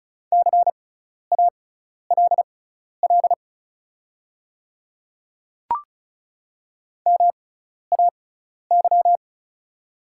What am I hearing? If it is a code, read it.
Morse code, heard as a keyed tone, about 36 wpm, spaced as for 12 wpm, 700 Hz. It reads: CALL MAY
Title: 6x Character Spacing - Top 300 Words - 35wpm